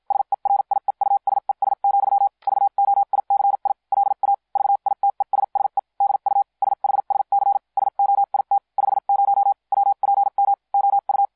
Morse code, heard as a keyed tone, about 50 wpm, spaced as for 43 wpm, 800 Hz. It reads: SERIEUSES, VOICI LA VITESSE DU SHSC SOIT 50 WPM OU
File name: cw shsc demo f6ddr